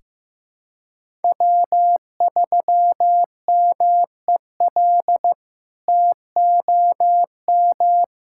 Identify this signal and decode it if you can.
Morse code, heard as a keyed tone, about 15 wpm, 700 Hz. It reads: W3MEL TOM
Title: K1USN SST QSO Practice - 15wpm